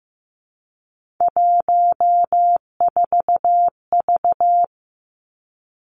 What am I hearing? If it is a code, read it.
Morse code, heard as a keyed tone, about 15 wpm, 700 Hz. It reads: 14V
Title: Three Challenging Letter-Number Combinations 15wpm